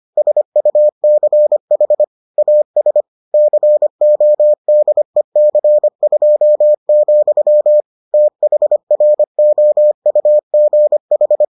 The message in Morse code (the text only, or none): SUCH AS CODEC2, THROUGH